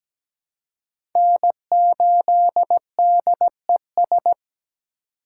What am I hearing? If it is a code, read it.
Morse code, heard as a keyed tone, about 20 wpm, 700 Hz. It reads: N8DES
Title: Speed Racing - USA 1x3 Call Signs 17wpm